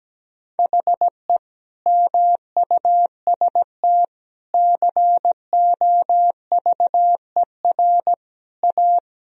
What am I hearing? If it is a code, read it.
Morse code, heard as a keyed tone, about 17 wpm, 700 Hz. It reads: HE MUST COVER A
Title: Sentences from Top 500 Words and Abbreviations 17wpm